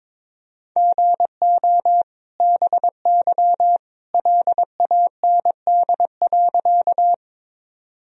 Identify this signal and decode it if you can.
Morse code, heard as a keyed tone, about 21 wpm, 700 Hz. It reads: GO BY LAND.
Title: Sentences from Top 1000 Words - Review Part 1 - 22wpm